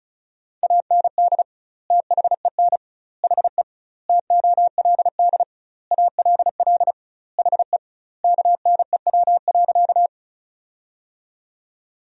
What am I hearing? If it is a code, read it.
Morse code, heard as a keyed tone, about 34 wpm, 700 Hz. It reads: AND THEN HE TOLD ALL HE KNEW.
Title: 2x Word-Spacing - Sentences from Top 400 Words - 35wpm